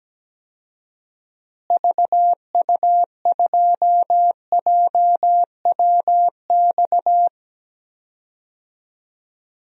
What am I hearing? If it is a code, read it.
Morse code, heard as a keyed tone, about 17 wpm, 700 Hz. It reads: VU2JWX
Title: DXCC-100 Call Signs - 6 Characters - Repeated 2x - 17wpm